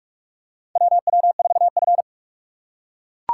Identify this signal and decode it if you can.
Morse code, heard as a keyed tone, about 43 wpm, 700 Hz. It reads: WW4F
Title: USA Call Signs - 4 Characters - 3x Character Spacing - 45wpm